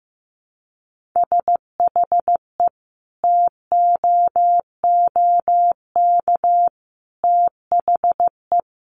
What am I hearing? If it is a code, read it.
Morse code, heard as a keyed tone, about 15 wpm, 700 Hz. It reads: SHE TOOK THE
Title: Sentences from Top 1400 Words 15wpm